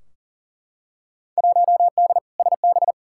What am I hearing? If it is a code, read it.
Morse code, heard as a keyed tone, about 40 wpm, 700 Hz. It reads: 1D SB
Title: Field Day Exchanges - 40wpm